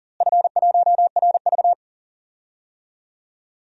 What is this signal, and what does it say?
Morse code, heard as a keyed tone, about 40 wpm, 700 Hz. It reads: F1RV
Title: DXCC-100 Call Signs - 4 Characters - Repeated 3x - 40wpm